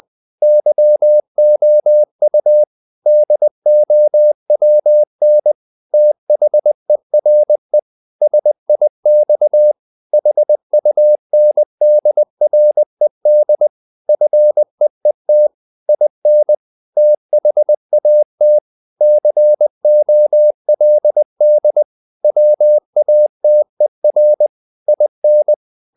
Morse code, heard as a keyed tone, about 20 wpm, 600 Hz. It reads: YOU DOWN THERE SIX HUNDRED FEET IN THAT COLD WATER IN
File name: the old man and the sea / the old man and the sea part 1 20wpm